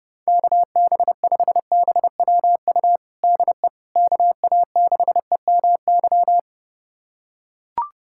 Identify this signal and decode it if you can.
Morse code, heard as a keyed tone, about 30 wpm, 700 Hz. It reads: KB5BWU DE KA6EMY
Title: USA Callsign DE Callsign - 6 Characters - 30wpm